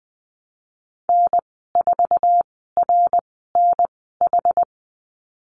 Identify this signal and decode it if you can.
Morse code, heard as a keyed tone, about 20 wpm, spaced as for 14 wpm, 700 Hz. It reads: N4RNH
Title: 062 Lesson 11g - Letter h - Call Signs 20wpm